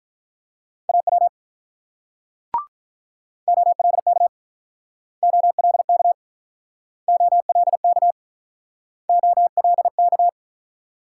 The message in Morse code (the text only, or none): AW OLK OLK OLK OLK